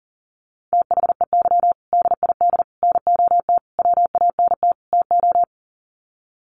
Morse code, heard as a keyed tone, about 40 wpm, 700 Hz. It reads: THEY DID NOT WANT TO
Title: Sets of 5 Words 40wpm